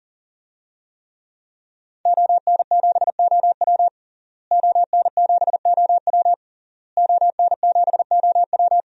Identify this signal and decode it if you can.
Morse code, heard as a keyed tone, about 37 wpm, 700 Hz. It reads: ON7OW ON7OW ON7OW
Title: DXCC-100 Call Signs - 5 Characters - Repeated 3x - 40wpm